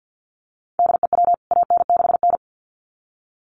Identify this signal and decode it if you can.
Morse code, heard as a keyed tone, about 50 wpm, 700 Hz. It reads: DEW AN6N
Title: CWT Exchange 50wpm